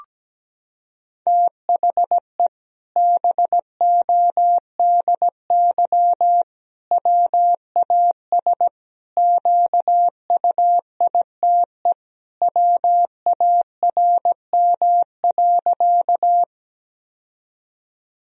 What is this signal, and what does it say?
Morse code, heard as a keyed tone, about 17 wpm, 700 Hz. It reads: THE BODY WAS QUITE WARM.
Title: Sentences from Top 1000 Words - Review Part 2 - 17wpm